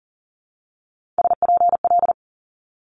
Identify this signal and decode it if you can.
Morse code, heard as a keyed tone, about 41 wpm, 700 Hz. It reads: SPL
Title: Multi-letter Phonemes 40wpm